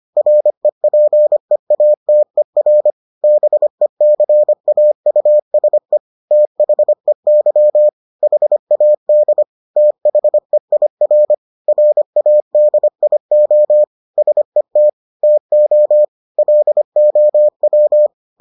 Morse code, heard as a keyed tone, about 25 wpm, 600 Hz. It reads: REPEATER BECAUSE THEY HAD THEIR RADIO SET TO LOW